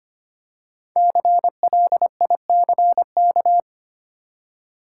Morse code, heard as a keyed tone, about 25 wpm, 700 Hz. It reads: CLICK
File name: 5 Letter Words - First 3 Letters Given - 25wpm